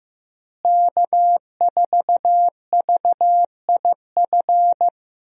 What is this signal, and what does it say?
Morse code, heard as a keyed tone, about 15 wpm, 700 Hz. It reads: K4VIF
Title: USA Call Signs - 5 Characters - Repeated 2x - 15wpm